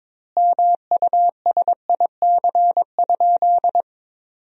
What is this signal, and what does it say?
Morse code, heard as a keyed tone, about 22 wpm, 700 Hz. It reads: MUSIC?
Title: 2x Word-Spacing - Sentences from Top 300 Words - 22wpm